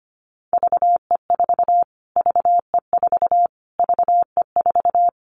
Morse code, heard as a keyed tone, about 25 wpm, 700 Hz. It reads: VE4 VE4 VE4
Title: Callsign Prefixes for Top 25 DXCC - Repeated 3x - 25wpm